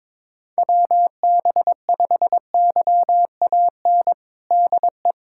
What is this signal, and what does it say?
Morse code, heard as a keyed tone, about 21 wpm, 700 Hz. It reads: WB5YAN DE
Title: USA Callsign DE Callsign 22wpm